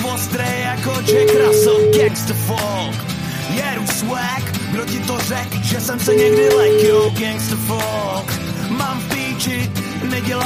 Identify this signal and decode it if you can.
Ringback tone (Europe, 425 Hz), with noise.